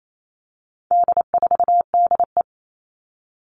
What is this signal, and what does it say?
Morse code, heard as a keyed tone, about 28 wpm, 700 Hz. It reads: D4DE
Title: All Cut Numbers - 4-Digit 28wpm